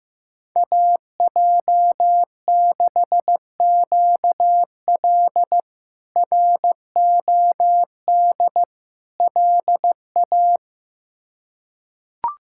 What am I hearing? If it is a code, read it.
Morse code, heard as a keyed tone, about 16 wpm, 700 Hz. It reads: AJ6QL ROD LA
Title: K1USN SST QSO Practice - 15wpm